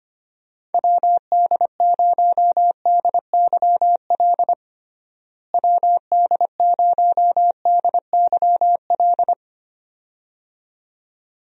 Morse code, heard as a keyed tone, about 25 wpm, 700 Hz. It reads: WD0DYL WD0DYL
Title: USA Call Signs - 6 Characters - Repeated 2x - 25wpm